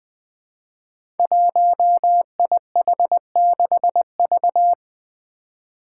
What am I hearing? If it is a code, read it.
Morse code, heard as a keyed tone, about 20 wpm, 700 Hz. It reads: 1IH6V
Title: Five Challenging Letter-Number Combinations 20wpm